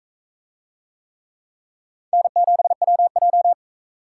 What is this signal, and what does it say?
Morse code, heard as a keyed tone, about 38 wpm, 700 Hz. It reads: N7WJ